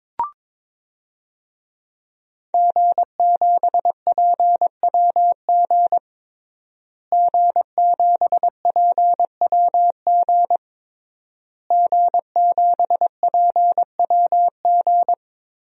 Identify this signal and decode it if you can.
Morse code, heard as a keyed tone, about 22 wpm, 700 Hz. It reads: G7PWG G7PWG G7PWG
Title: DXCC-100 Call Signs - 5 Characters - Repeated 3x - 22wpm